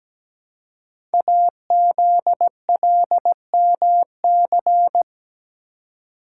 Morse code, heard as a keyed tone, about 17 wpm, 700 Hz. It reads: AZLMC
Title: Code Groups - 5 Letters - 17wpm